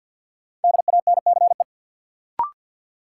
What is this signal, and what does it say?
Morse code, heard as a keyed tone, about 52 wpm, 700 Hz. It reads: DANCE
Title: Sentences from Top 700 Words - New Word Drill - 50wpm